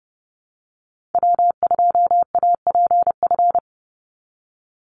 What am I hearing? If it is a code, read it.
Morse code, heard as a keyed tone, about 30 wpm, 700 Hz. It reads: W2APF
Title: Call Signs Repeated 2x - 30wpm